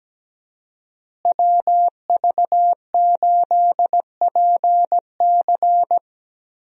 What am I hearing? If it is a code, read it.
Morse code, heard as a keyed tone, about 17 wpm, 700 Hz. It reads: WV8PC
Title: USA Call Signs - 5 Characters - Repeated 3x - 17wpm